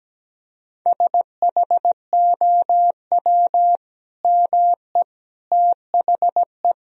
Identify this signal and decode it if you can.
Morse code, heard as a keyed tone, about 17 wpm, 700 Hz. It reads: SHOW ME THE
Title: Sentences from Top 1000 Words - Review Part 1 - 17wpm